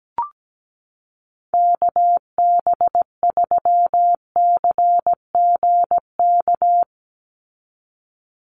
Morse code, heard as a keyed tone, about 17 wpm, 700 Hz. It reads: KB3CGK
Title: USA Call Signs 17wpm